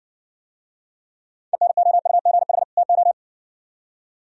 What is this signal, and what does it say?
Morse code, heard as a keyed tone, about 67 wpm, 700 Hz. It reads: ENOUGH TO